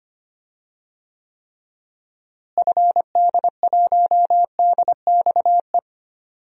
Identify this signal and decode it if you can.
Morse code, heard as a keyed tone, about 25 wpm, 700 Hz. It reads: FD1DXE
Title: DXCC-100 Call Signs - 6 Characters - Repeated 3x - 25wpm